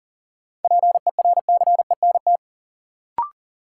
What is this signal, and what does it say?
Morse code, heard as a keyed tone, about 40 wpm, 700 Hz. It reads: PERCENT
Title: Sets of 2 Words - New Word Drill - 40wpm